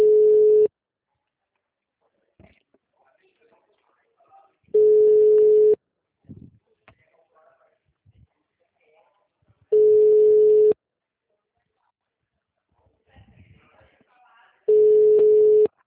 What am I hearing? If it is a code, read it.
Ringback tone (Europe, 425 Hz), with speech.